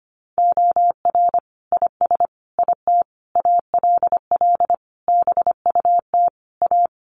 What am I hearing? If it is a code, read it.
Morse code, heard as a keyed tone, about 25 wpm, 700 Hz. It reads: OR IS IT ALL BUT A
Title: Sentences from Top 1500 Words 25wpm